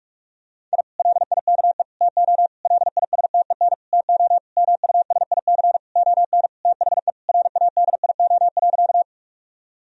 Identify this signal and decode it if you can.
Morse code, heard as a keyed tone, about 46 wpm, 700 Hz. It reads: I LIKE TO LISTEN TO MUSIC ON THE RADIO.
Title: Sentences from Top 1000 Words - Review Part 3 - 45wpm